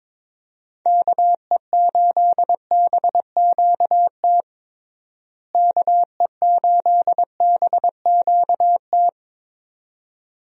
Morse code, heard as a keyed tone, about 22 wpm, 700 Hz. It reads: KE8BQT KE8BQT
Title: USA Call Signs - 6 Characters - Repeated 2x - 22wpm